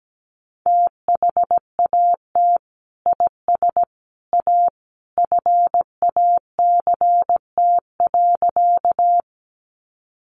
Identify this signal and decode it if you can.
Morse code, heard as a keyed tone, about 17 wpm, 700 Hz. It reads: THAT IS A FACT.
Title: Sentences from Top 500 Words 17wpm